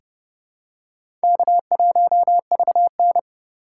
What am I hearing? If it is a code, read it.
Morse code, heard as a keyed tone, about 30 wpm, 700 Hz. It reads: K1VN
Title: USA Call Signs - 4 Characters - 2x Character Spacing - 30wpm